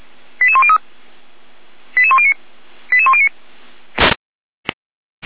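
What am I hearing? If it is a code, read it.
Five-tone selective call (ZVEI), heard as quick runs of five tones, tones of about 70 ms. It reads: 88103 88109 88109
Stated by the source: ZVEI five-tone selective call 88107, received by SDR, on the Feuerwehr Orlamünde audio server.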